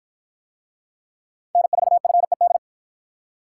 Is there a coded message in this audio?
Yes: Morse code from 1.6 to 2.6 s, heard as a keyed tone, about 53 wpm, 700 Hz. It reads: N4FED